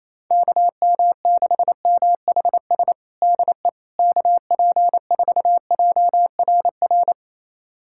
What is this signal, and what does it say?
Morse code, heard as a keyed tone, about 28 wpm, 700 Hz. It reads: KM6MHS DE KP4JRR
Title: USA Callsign DE Callsign - 6 Characters - 28wpm